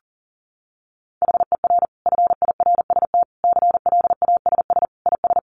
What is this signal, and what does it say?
Morse code, heard as a keyed tone, about 40 wpm, 700 Hz. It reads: HER FIRST CLASS IS
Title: Additional Easy Phrases 40wpm